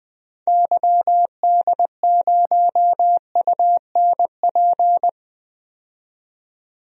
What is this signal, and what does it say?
Morse code, heard as a keyed tone, about 20 wpm, 700 Hz. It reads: YD0UNP
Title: DXCC-100 Call Signs - 6 Characters - 20wpm